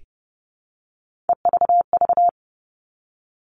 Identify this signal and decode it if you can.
Morse code, heard as a keyed tone, about 30 wpm, 700 Hz. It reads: EVV
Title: All Cut Numbers - 3-Digit 30wpm